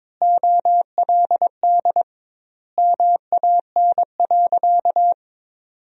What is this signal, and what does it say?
Morse code, heard as a keyed tone, about 22 wpm, 700 Hz. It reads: OLD MAN.
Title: 2x Word-Spacing - Sentences from Top 500 Words - 22wpm